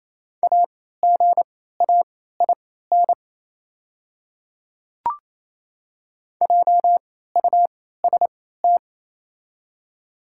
Morse code, heard as a keyed tone, about 28 wpm, spaced as for 16 wpm, 700 Hz. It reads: AGAIN JUST
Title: 3x Character Spacing - Top 200 Words - 28wpm